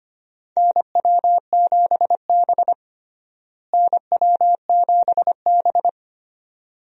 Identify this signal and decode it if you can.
Morse code, heard as a keyed tone, about 25 wpm, 700 Hz. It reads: NW7B NW7B
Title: USA Call Signs - 4 Characters - Repeated 2x - 25wpm